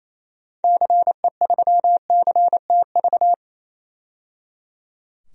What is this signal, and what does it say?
Morse code, heard as a keyed tone, about 28 wpm, 700 Hz. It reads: CE3CTV
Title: DXCC-100 Call Signs - 6 Characters - 28wpm